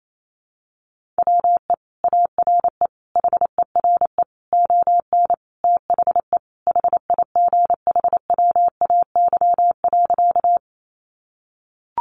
Morse code, heard as a keyed tone, about 28 wpm, 700 Hz. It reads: WE ARE HERE ON THE HIGHWAY.
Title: Sentences from Top 1400 Words 28wpm